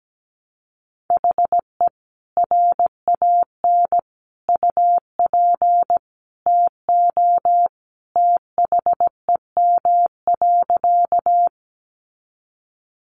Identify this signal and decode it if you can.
Morse code, heard as a keyed tone, about 17 wpm, 700 Hz. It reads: HE RAN UP TO THEM.
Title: Sentences from Top 500 Words 17wpm